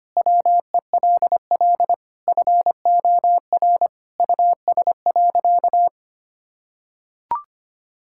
Morse code, heard as a keyed tone, about 25 wpm, 700 Hz. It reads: WELL FOR US.
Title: Sentences from Top 1000 Words - Review Part 6 - 25wpm